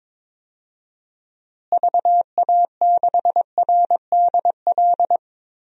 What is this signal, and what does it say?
Morse code, heard as a keyed tone, about 22 wpm, 700 Hz. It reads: VA6RDL
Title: DXCC-100 Call Signs - 6 Characters - Repeated 3x - 22wpm